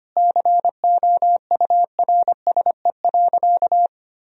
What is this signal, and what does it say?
Morse code, heard as a keyed tone, about 25 wpm, 700 Hz. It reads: COURSE.
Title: Sentences from Top 1000 Words - Review Part 3 - 25wpm